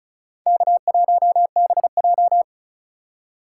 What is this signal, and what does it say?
Morse code, heard as a keyed tone, about 36 wpm, 700 Hz. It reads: K1BJ
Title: USA Call Signs - 4 Characters - 3x Character Spacing - 35wpm